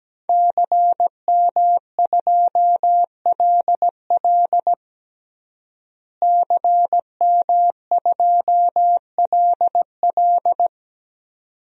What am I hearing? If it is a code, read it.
Morse code, heard as a keyed tone, about 17 wpm, 700 Hz. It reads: CM2LL CM2LL
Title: DXCC-100 Call Signs - 5 Characters - Repeated 2x - 17wpm